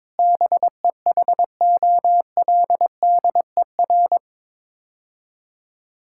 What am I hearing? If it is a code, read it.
Morse code, heard as a keyed tone, about 22 wpm, 700 Hz. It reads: BEHOLDER
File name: practice / 22wpm / Sets of 7 Words - 22wpm